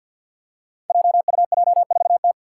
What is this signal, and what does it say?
Morse code, heard as a keyed tone, about 49 wpm, 700 Hz. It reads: JUJ4T
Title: Five Challenging Letter-Number Combinations 50wpm